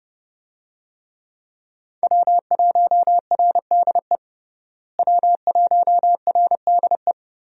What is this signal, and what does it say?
Morse code, heard as a keyed tone, about 30 wpm, 700 Hz. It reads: W1RDE W1RDE
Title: USA Call Signs - 5 Characters - Repeated 2x - 30wpm